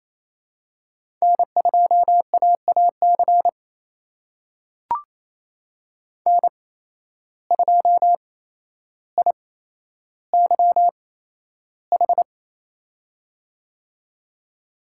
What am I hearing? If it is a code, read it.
Morse code, heard as a keyed tone, about 28 wpm, spaced as for 12 wpm, 700 Hz. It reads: N2AAC N2IYH
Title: USA Call Signs - 5 Characters - 8x Character Spacing - 28wpm